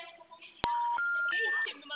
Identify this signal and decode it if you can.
Special information tone, with speech.
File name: out- -2007- - - .9737